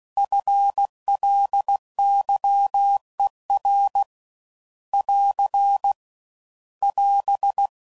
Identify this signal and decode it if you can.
Morse code, heard as a keyed tone, about 16 wpm, 800 Hz. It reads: FLYER <AR> <AS>